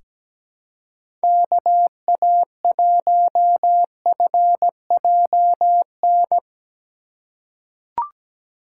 Morse code, heard as a keyed tone, about 17 wpm, 700 Hz. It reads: KA1FJN
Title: USA Call Signs - 6 Characters - 2x Character Spacing - 17wpm